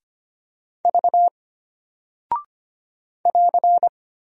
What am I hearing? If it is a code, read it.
Morse code, heard as a keyed tone, about 25 wpm, 700 Hz. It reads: V <AR>
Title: Single Character (Letter:Number:Punctuation:Prosign) Alt 25wpm